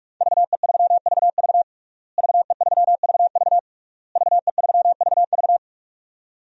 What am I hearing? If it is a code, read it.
Morse code, heard as a keyed tone, about 46 wpm, 700 Hz. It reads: VE3VV VE3VV VE3VV
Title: DXCC-100 Call Signs - 5 Characters - Repeated 3x - 45wpm